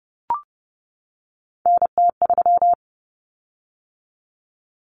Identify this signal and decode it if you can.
Morse code, heard as a keyed tone, about 30 wpm, 700 Hz. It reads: NT3
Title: Common Cut Numbers - 3-Digit 30wpm